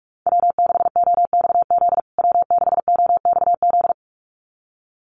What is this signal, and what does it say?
Morse code, heard as a keyed tone, about 44 wpm, 690 Hz. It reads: W6OXZ W6OXZ
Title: USA Call Signs Repeated 2x 45wpm